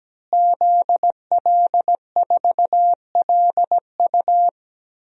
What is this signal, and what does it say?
Morse code, heard as a keyed tone, about 17 wpm, 700 Hz. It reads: ZL4LU
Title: DXCC-100 Call Signs - 5 Characters - Repeated 3x - 17wpm